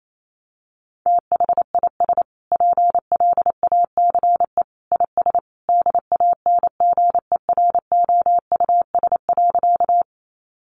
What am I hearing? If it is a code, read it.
Morse code, heard as a keyed tone, about 28 wpm, 700 Hz. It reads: THIS PLACE IS DANGEROUS.
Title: Sentences from Top 1200 Words 28wpm